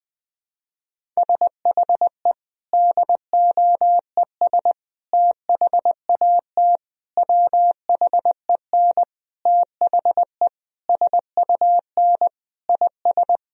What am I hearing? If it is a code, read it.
Morse code, heard as a keyed tone, about 20 wpm, 700 Hz. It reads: SHE DOES THAT WHEN THE SUN IS
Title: Sentences from Top 1000 Words - Review Part 1 - 20wpm